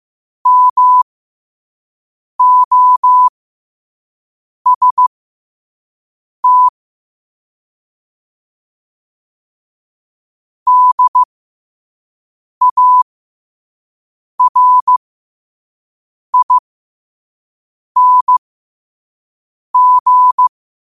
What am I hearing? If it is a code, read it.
Morse code, heard as a keyed tone, about 15 wpm, spaced as for 5 wpm, 1000 Hz. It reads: MOST DARING